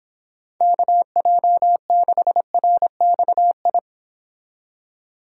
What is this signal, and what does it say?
Morse code, heard as a keyed tone, about 26 wpm, 700 Hz. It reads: KJ6RXI